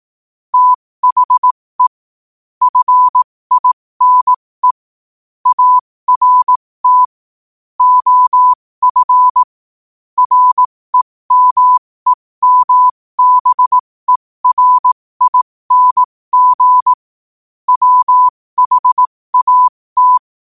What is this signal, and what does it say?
Morse code, heard as a keyed tone, about 18 wpm, spaced as for 15 wpm, 1000 Hz. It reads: THE FINE ART OF REMEMBERING WHAT
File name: qotdfiles / 15wpm